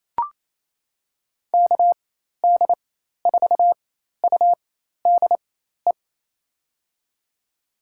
Morse code, heard as a keyed tone, about 27 wpm, spaced as for 13 wpm, 700 Hz. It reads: KD4UDE
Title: USA Call Signs - 6 Characters - 4x Character Spacing - 28wpm